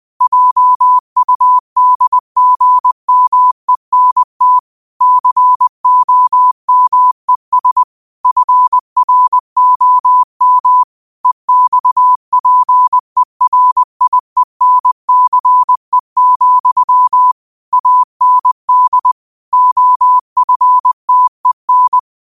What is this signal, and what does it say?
Morse code, heard as a keyed tone, about 20 wpm, 1000 Hz. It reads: JUDGMENT COMES FROM EXPERIENCE, AND OFTEN
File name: qotdfiles / 20wpm